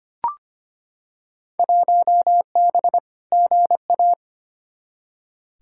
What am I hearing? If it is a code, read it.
Morse code, heard as a keyed tone, about 25 wpm, 700 Hz. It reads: 1B GA
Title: Field Day Exchanges Encore - 25wpm